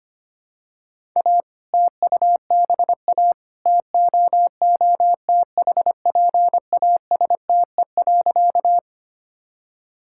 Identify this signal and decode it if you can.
Morse code, heard as a keyed tone, about 25 wpm, 700 Hz. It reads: A TUBA TOOTHPASTE.
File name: practice / 25wpm / All-Age Appropriate Jokes - 25wpm